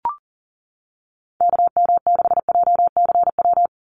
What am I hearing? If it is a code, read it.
Morse code, heard as a keyed tone, about 40 wpm, 700 Hz. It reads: KM6JCW
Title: USA Call Signs 40wpm